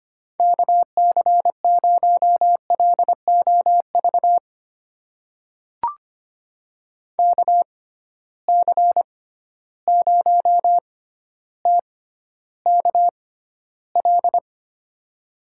Morse code, heard as a keyed tone, about 25 wpm, spaced as for 13 wpm, 700 Hz. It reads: KC0LOV KC0TKL
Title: USA Call Signs - 6 Characters - 6x Character Spacing - 25wpm